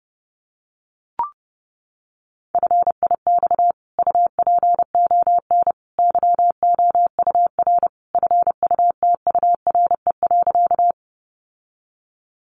Morse code, heard as a keyed tone, about 30 wpm, 700 Hz. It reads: FIX UPON YOUR FUTURE.
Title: Sentences from Top 1400 Words 30wpm